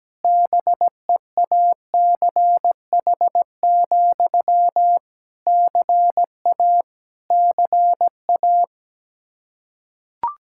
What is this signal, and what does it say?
Morse code, heard as a keyed tone, about 17 wpm, 700 Hz. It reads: BEACH, CA CA
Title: K1USN SST QSO Practice 17wpm